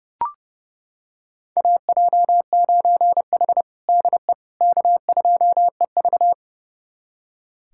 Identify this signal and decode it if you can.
Morse code, heard as a keyed tone, about 30 wpm, 700 Hz. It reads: AJ9H DE K2EV
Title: USA Callsign DE Callsign - 4 Characters - 30wpm